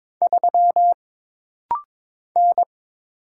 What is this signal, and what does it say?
Morse code, heard as a keyed tone, about 22 wpm, 700 Hz. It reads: 3 N
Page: This rapid-fire timing is geared to push your ICR (Instant Character Recognition) skills.